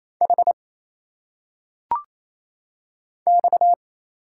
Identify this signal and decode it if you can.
Morse code, heard as a keyed tone, about 28 wpm, 700 Hz. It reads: H X
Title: Single Letters - Phonetic Pronunciation 28wpm